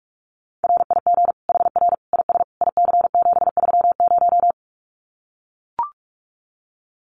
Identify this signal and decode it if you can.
Morse code, heard as a keyed tone, about 44 wpm, 700 Hz. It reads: RIG HR IS IC730
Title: Common QSO Phrases 45wpm